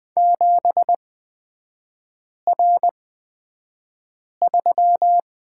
Morse code, heard as a keyed tone, about 20 wpm, spaced as for 5 wpm, 700 Hz. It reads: 7R3
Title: 224 Lesson 36e - BT - Three Characters 20wpm